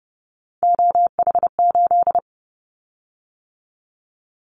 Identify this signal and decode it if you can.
Morse code, heard as a keyed tone, about 30 wpm, 700 Hz. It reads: OH8
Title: Callsign Prefixes for Top 25 DXCC - 30wpm